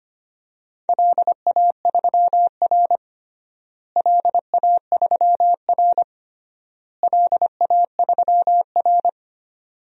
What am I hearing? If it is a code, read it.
Morse code, heard as a keyed tone, about 25 wpm, 700 Hz. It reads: LA3R LA3R LA3R
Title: DXCC-100 Call Signs - 4 Characters - Repeated 3x - 25wpm